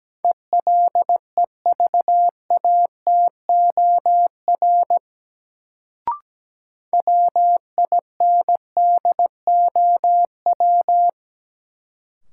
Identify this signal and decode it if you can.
Morse code, heard as a keyed tone, about 17 wpm, 700 Hz. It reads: ELEVATOR WINDOW
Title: Sentences from Top 1600 Words - New Word Drill - 17wpm